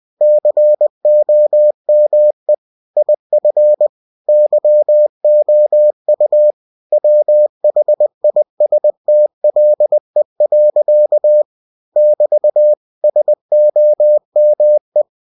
Morse code, heard as a keyed tone, about 20 wpm, 600 Hz. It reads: COME IF YOU WHISTLE. <BT> SOME
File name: cw-books / winnie-the-pooh / winnie-20wpm-page-0038